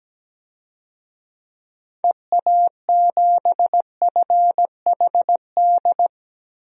Morse code, heard as a keyed tone, about 17 wpm, 700 Hz. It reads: EA7FHD
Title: DXCC-100 Call Signs - 6 Characters - 17wpm